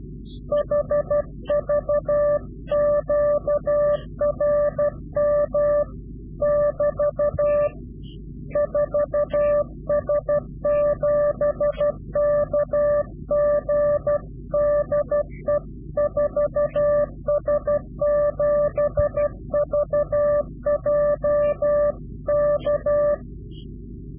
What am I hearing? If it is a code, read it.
Morse code, heard as a keyed tone, about 12 wpm, 590 Hz. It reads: HV QRM <BT> 4S7KG DE 4S7VJ K
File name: CW-QSO / Directional Call / Directional QSO